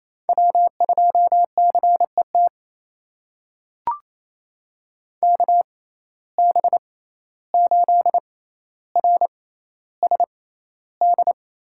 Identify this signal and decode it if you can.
Morse code, heard as a keyed tone, about 28 wpm, spaced as for 14 wpm, 700 Hz. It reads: W2CET KB8RSD
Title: USA Call Signs - 6x Character Spacing - 28wpm